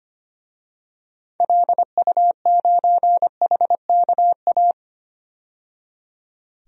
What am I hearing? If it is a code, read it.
Morse code, heard as a keyed tone, about 25 wpm, 700 Hz. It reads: LU9HKA
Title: DXCC-100 Call Signs - 6 Characters - 25wpm